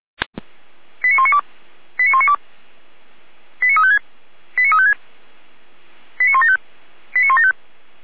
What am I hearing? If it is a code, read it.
Five-tone selective call (ZVEI), heard as quick runs of five tones, tones of about 70 ms. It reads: 89192 89192 89356 89356 89175 89175